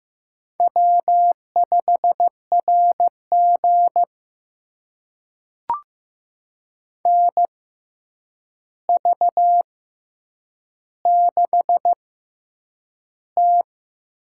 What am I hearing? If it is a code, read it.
Morse code, heard as a keyed tone, about 15 wpm, spaced as for 8 wpm, 700 Hz. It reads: W5RG NV6T
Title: USA Call Signs - 4 Characters - 6x Character Spacing - 15wpm